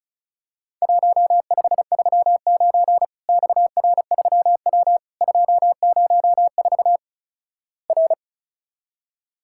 Morse code, heard as a keyed tone, about 36 wpm, 700 Hz. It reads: 1539 XR3W 204 R
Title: Worked All Europe DX Contest - QTC Exchange - 35wpm